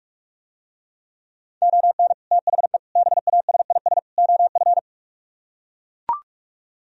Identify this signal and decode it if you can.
Morse code, heard as a keyed tone, about 44 wpm, 700 Hz. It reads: ON THE BASIS OF